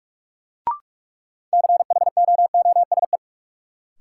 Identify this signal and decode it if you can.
Morse code, heard as a keyed tone, about 47 wpm, 700 Hz. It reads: CHOOSE
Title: Sentences from Top 700 Words - New Word Drill - 45wpm